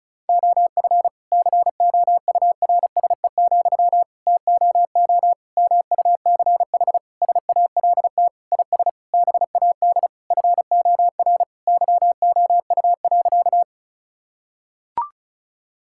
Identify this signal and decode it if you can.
Morse code, heard as a keyed tone, about 34 wpm, 700 Hz. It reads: OF COURSE, TOO MUCH SALT IS BAD FOR YOU.
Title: Sentences from Top 1000 Words - Review Part 5 - 35wpm